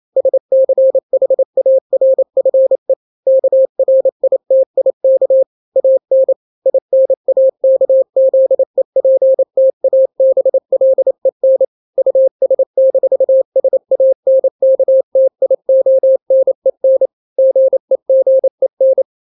Morse code, heard as a keyed tone, about 28 wpm, 530 Hz. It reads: SCHARFE KRITIK AN INAKZEPTABLEN US-SANKTIONEN GEGEN
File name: daily / 27wpm